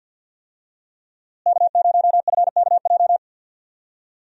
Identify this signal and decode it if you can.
Morse code, heard as a keyed tone, about 51 wpm, 700 Hz. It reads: K0FKJ
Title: USA Call Signs - 5 Characters - Repeated 2x - 50wpm